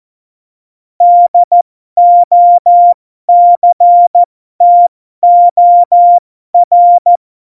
Morse code, heard as a keyed tone, about 14 wpm, spaced as for 12 wpm, 700 Hz. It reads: DOCTOR
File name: Morsekurs / Witze